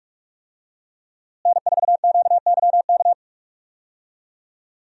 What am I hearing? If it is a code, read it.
Morse code, heard as a keyed tone, about 46 wpm, 700 Hz. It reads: N4QYK
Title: USA Call Signs - 5 Characters - Repeated 2x - 45wpm